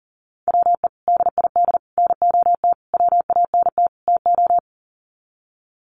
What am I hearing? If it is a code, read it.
Morse code, heard as a keyed tone, about 37 wpm, 700 Hz. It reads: WE DID NOT WANT TO
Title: Sets of 5 Words Encore 40wpm